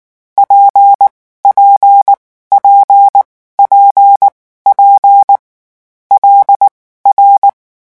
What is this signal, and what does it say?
Morse code, heard as a keyed tone, about 19 wpm, spaced as for 14 wpm, 800 Hz. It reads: PPP PP L R
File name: morse mp3 / lecon6-vitesse 18